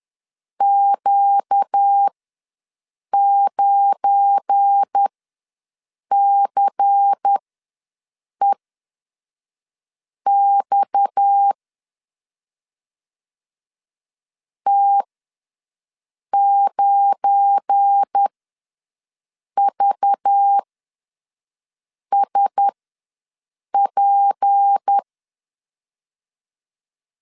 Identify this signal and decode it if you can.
Morse code, heard as a keyed tone, about 10 wpm, spaced as for 5 wpm, 800 Hz. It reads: Q9CEX T9VSP
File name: LEZIONI / 7 GF9Z / 7 GF9 ws 10wpm 4034